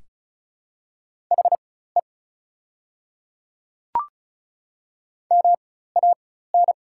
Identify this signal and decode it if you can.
Morse code, heard as a keyed tone, about 36 wpm, spaced as for 16 wpm, 700 Hz. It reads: HE MAN
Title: 4x Character Spacing - Top 200 Words - 35wpm